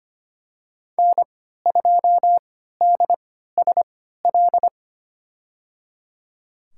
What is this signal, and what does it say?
Morse code, heard as a keyed tone, about 25 wpm, spaced as for 14 wpm, 700 Hz. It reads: N2DSL
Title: USA Call Signs - 5 Characters - 3x Character Spacing - 25wpm